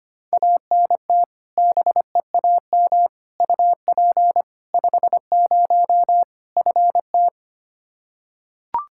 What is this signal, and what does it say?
Morse code, heard as a keyed tone, about 25 wpm, 700 Hz. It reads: ANT BEAM UP 50 FT
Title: Practice QSO - Exchange 2 - Info - 25wpm